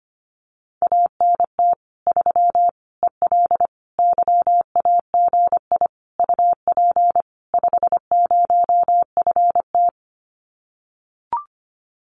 Morse code, heard as a keyed tone, about 25 wpm, 700 Hz. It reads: ANT 3 EL YAGI UP 50FT
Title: Common QSO Phrases Encore 25wpm